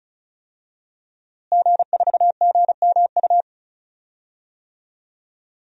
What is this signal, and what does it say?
Morse code, heard as a keyed tone, about 36 wpm, 700 Hz. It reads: G4GMU
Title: DXCC-100 Call Signs - 5 Characters - Repeated 3x - 35wpm